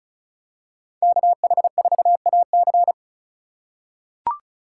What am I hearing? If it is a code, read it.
Morse code, heard as a keyed tone, about 36 wpm, 700 Hz. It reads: KH4AC
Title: USA Call Signs - 4x Character Spacing - 35wpm